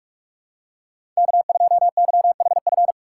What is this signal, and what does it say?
Morse code, heard as a keyed tone, about 46 wpm, 700 Hz. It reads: K2YHF
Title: USA Call Signs - 5 Characters - 6x Character Spacing - 45wpm